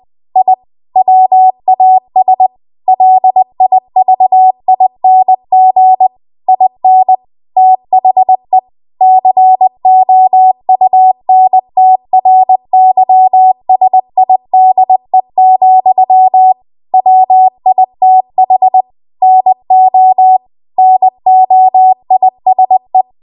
Morse code, heard as a keyed tone, about 20 wpm, 750 Hz. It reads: I WAS LIVING IN THE COUNTRYSIDE, WITH NO NOISE
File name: Morse / 20WPM